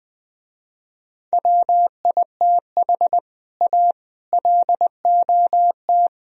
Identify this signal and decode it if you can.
Morse code, heard as a keyed tone, about 20 wpm, 700 Hz. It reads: WITH A LOT
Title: Speed-Racing - Sets of 4 Words 20wpm